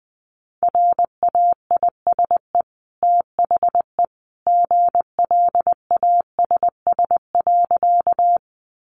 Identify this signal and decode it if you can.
Morse code, heard as a keyed tone, about 20 wpm, 700 Hz. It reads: RAISE THE GLASS.
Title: Sentences from Top 1400 Words 20wpm